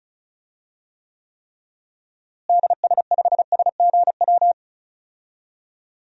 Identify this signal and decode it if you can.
Morse code, heard as a keyed tone, about 36 wpm, 700 Hz. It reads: DS5SGW
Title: DXCC-100 Call Signs - 6 Characters - Repeated 3x - 35wpm